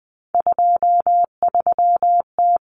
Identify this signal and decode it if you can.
Morse code, heard as a keyed tone, about 20 wpm, 700 Hz. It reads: 23T
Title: Common Cut Numbers - 3-Digit 20wpm